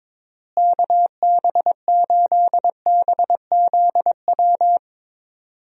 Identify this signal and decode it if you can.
Morse code, heard as a keyed tone, about 22 wpm, 700 Hz. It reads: KB8BZW
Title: USA Call Signs - 6 Characters - Repeated 3x - 22wpm